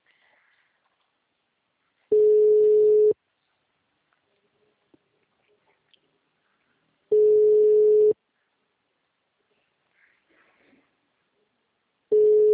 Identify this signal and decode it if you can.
Ringback tone (Europe, 425 Hz).